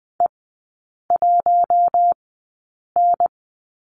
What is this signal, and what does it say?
Morse code, heard as a keyed tone, about 20 wpm, spaced as for 8 wpm, 700 Hz. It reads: E1N
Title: 041 Lesson 8e - Number 1 - Three Characters 20wpm